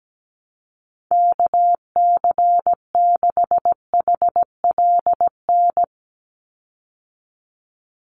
Morse code, heard as a keyed tone, about 17 wpm, 700 Hz. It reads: KC6HLN
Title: USA Call Signs 17wpm